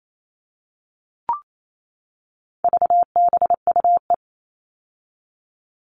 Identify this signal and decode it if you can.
Morse code, heard as a keyed tone, about 28 wpm, 700 Hz. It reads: VBUE
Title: All Cut Numbers - 4-Digit 28wpm